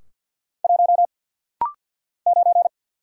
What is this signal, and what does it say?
Morse code, heard as a keyed tone, about 50 wpm, 700 Hz. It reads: 1 9
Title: Single Letter-Number Alt - Rapid-Fire 50wpm